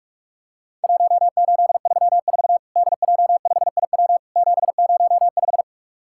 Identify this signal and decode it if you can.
Morse code, heard as a keyed tone, about 44 wpm, 700 Hz. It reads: 1834 DJ5IW 705
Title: Worked All Europe DX Contest - QTC Exchange - 45wpm